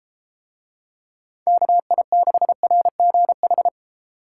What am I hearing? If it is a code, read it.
Morse code, heard as a keyed tone, about 32 wpm, 700 Hz. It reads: KI6RGH